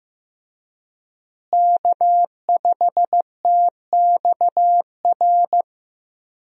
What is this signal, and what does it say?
Morse code, heard as a keyed tone, about 15 wpm, 700 Hz. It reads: K5TXR
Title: USA Call Signs - 5 Characters - Repeated 2x - 15wpm